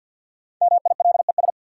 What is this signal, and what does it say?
Morse code, heard as a keyed tone, about 51 wpm, 700 Hz. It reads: MILES
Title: Sentences from Top 1000 Words - Review Part 3 - New Word Drill - 50wpm